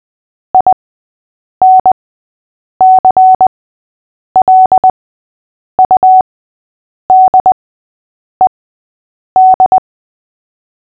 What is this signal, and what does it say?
Morse code, heard as a keyed tone, about 20 wpm, spaced as for 8 wpm, 750 Hz. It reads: INCLUDED
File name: cw / fists / Short-Story-3-8wpm